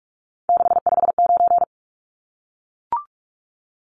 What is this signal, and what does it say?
Morse code, heard as a keyed tone, about 44 wpm, 700 Hz. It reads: 659
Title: Numbers - 3-Digit 45wpm